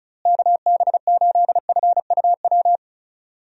Morse code, heard as a keyed tone, about 36 wpm, 700 Hz. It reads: KB8FUW
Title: USA Call Signs - 6 Characters - Repeated 3x - 35wpm